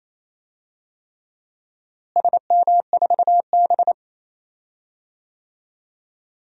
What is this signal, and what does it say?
Morse code, heard as a keyed tone, about 28 wpm, 700 Hz. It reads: SM4B
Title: DXCC-100 Call Signs - 4 Characters - Repeated 3x - 28wpm